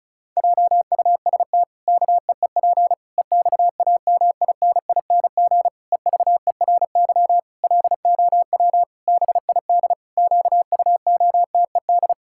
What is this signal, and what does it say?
Morse code, heard as a keyed tone, about 34 wpm, 700 Hz. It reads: JUST KEEP EXAMINING EVERY LOW BID QUOTED
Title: Pangram Challenge v2 - 35wpm